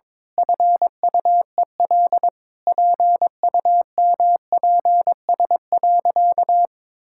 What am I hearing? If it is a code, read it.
Morse code, heard as a keyed tone, about 22 wpm, 700 Hz. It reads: FUEL PUMPS.
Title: Pangram Challenge v2 - 22wpm